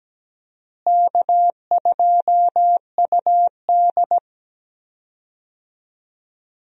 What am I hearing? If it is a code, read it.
Morse code, heard as a keyed tone, about 17 wpm, 700 Hz. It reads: K2UD
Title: USA Call Signs - 4 Characters - Repeated 2x - 17wpm